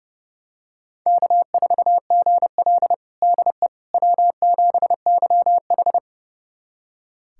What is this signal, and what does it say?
Morse code, heard as a keyed tone, about 31 wpm, 700 Hz. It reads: K4GL DE W7YH
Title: USA Callsign DE Callsign - 4 Characters - 30wpm